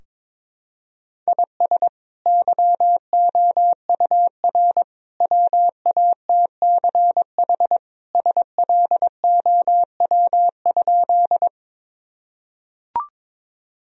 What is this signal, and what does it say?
Morse code, heard as a keyed tone, about 22 wpm, 700 Hz. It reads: IS YOUR WATCH SLOW?
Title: Sentences from Top 1000 Words - Review Part 2 - 22wpm